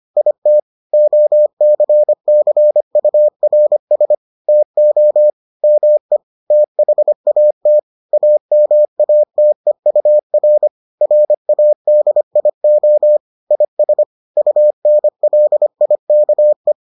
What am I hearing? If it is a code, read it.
Morse code, heard as a keyed tone, about 25 wpm, 600 Hz. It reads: IT OCCURS TO ME THAT AMATEUR RADIO IS UNLIKE